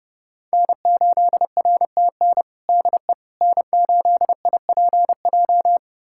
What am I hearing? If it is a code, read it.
Morse code, heard as a keyed tone, about 30 wpm, 700 Hz. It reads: N8RTN DE N8IPJ
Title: USA Callsign DE Callsign - 5 Characters - 30wpm